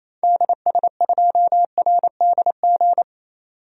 Morse code, heard as a keyed tone, about 28 wpm, 700 Hz. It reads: DS2RDG
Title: DXCC-100 Call Signs - 6 Characters - 28wpm